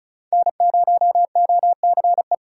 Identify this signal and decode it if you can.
Morse code, heard as a keyed tone, about 36 wpm, 700 Hz. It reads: N0OCE
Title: USA Call Signs - 5 Characters - 3x Character Spacing - 35wpm